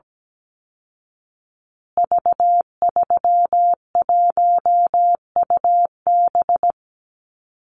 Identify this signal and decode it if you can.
Morse code, heard as a keyed tone, about 17 wpm, 700 Hz. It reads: V31UB
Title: Call Signs Repeated 2x - 17wpm